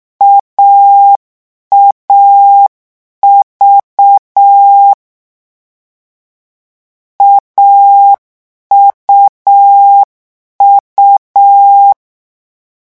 Morse code, heard as a keyed tone, about 6 wpm, 800 Hz. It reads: AAV AUU
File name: morse mp3 / lecon3-vitesse 6